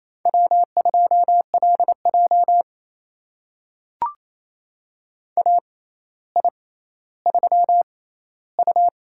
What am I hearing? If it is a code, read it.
Morse code, heard as a keyed tone, about 28 wpm, spaced as for 14 wpm, 700 Hz. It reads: W2LJ AI3U
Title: USA Call Signs - 4 Characters - 6x Character Spacing - 28wpm